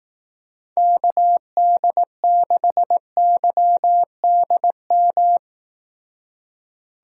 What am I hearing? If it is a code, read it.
Morse code, heard as a keyed tone, about 18 wpm, 700 Hz. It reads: KD6YDM
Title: Speed Racing - USA Call Signs 15wpm